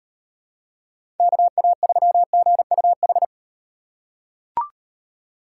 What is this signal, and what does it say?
Morse code, heard as a keyed tone, about 38 wpm, 700 Hz. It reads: KA3GUH